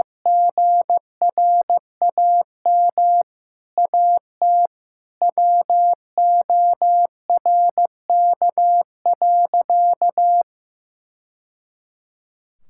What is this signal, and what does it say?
Morse code, heard as a keyed tone, about 15 wpm, 700 Hz. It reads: GRAM AT WORK.
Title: Sentences from Top 1000 Words - Review Part 5 - 15wpm